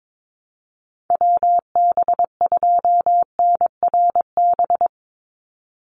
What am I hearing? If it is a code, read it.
Morse code, heard as a keyed tone, about 22 wpm, 700 Hz. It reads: WB2NRB
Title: USA Call Signs 22wpm